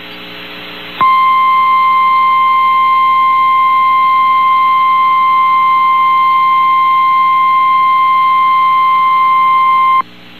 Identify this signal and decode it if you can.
Emergency Alert System (SAME) tones (the 1050 Hz attention tone). It is heard from 1.0 to 10.0 s.
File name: eas received alerts / L1 19473 audio